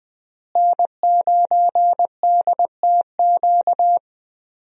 Morse code, heard as a keyed tone, about 20 wpm, 700 Hz. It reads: N9DTQ
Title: USA Call Signs - 5 Characters - Repeated 3x - 20wpm